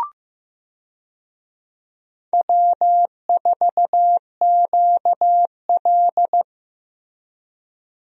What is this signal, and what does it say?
Morse code, heard as a keyed tone, about 15 wpm, 700 Hz. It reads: W4QL
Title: USA Call Signs - 4 Characters - 15wpm